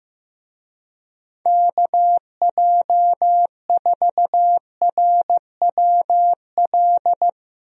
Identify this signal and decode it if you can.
Morse code, heard as a keyed tone, about 16 wpm, 700 Hz. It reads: KJ4RWL
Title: USA Call Signs - 6 Characters - Repeated 3x - 15wpm